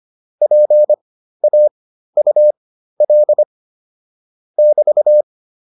Morse code, heard as a keyed tone, about 25 wpm, spaced as for 13 wpm, 600 Hz. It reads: PAUL <BT>
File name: QSO 407 13